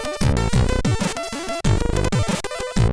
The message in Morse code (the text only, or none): none